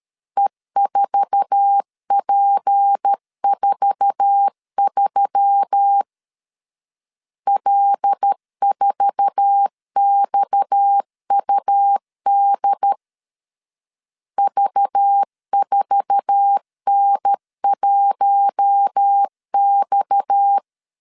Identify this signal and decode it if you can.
Morse code, heard as a keyed tone, about 13 wpm, 800 Hz. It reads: E4P43 L4XUD V4N1X
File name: LEZIONI / 8 YJ45 / 8 YJ4 12wpm 2151